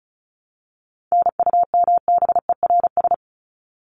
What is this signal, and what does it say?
Morse code, heard as a keyed tone, about 34 wpm, 700 Hz. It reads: NUMBERS
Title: ES:S Suffix - Top 100 Words with Base Words 35wpm